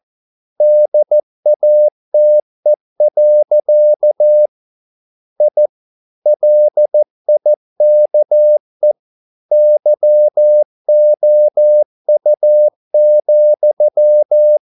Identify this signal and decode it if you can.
Morse code, heard as a keyed tone, about 14 wpm, 600 Hz. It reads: DATE. I LIKE YOU,